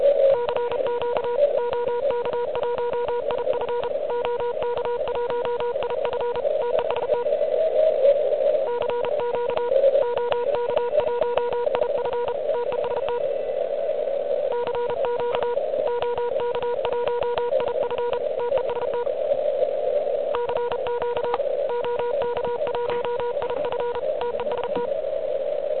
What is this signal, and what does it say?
Morse code, heard as a keyed tone, about 32 wpm, 970 Hz, with noise. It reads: CQ OK1IF OK1IF TEST CQ OK1IF TEST CQ OK1IF TEST CQ OK1IF TEST
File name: ARCHIV / QCX / QCX Pileup